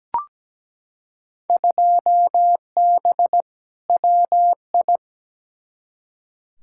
Morse code, heard as a keyed tone, about 17 wpm, 700 Hz. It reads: 2B WI
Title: Field Day Exchanges - 17wpm